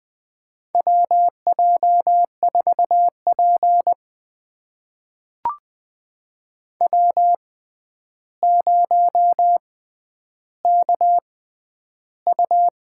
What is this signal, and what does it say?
Morse code, heard as a keyed tone, about 20 wpm, spaced as for 10 wpm, 700 Hz. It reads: WJ4P W0KU
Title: USA Call Signs - 6x Character Spacing - 20wpm